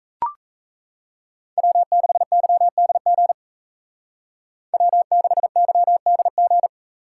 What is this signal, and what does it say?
Morse code, heard as a keyed tone, about 33 wpm, 700 Hz. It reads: W6YDG W6YDG
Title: Speed Racing - USA 1x3 Call Signs 28wpm